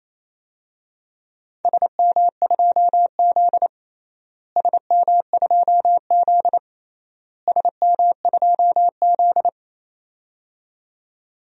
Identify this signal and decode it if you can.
Morse code, heard as a keyed tone, about 28 wpm, 700 Hz. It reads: SM2Z SM2Z SM2Z
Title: DXCC-100 Call Signs - 4 Characters - Repeated 3x - 28wpm